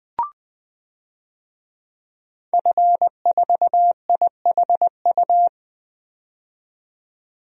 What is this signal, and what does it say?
Morse code, heard as a keyed tone, about 20 wpm, 700 Hz. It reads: F4IHU
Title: DXCC-100 Call Signs - 5 Characters - 20wpm